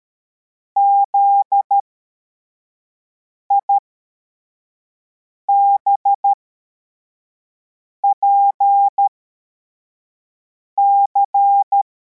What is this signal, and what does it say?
Morse code, heard as a keyed tone, about 13 wpm, spaced as for 4 wpm, 800 Hz. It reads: ZIBPC
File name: formation / cw / Text81